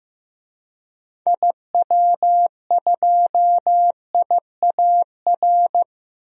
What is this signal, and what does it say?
Morse code, heard as a keyed tone, about 15 wpm, 700 Hz. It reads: IW2IAR
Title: DXCC-100 Call Signs - 6 Characters - Repeated 2x - 15wpm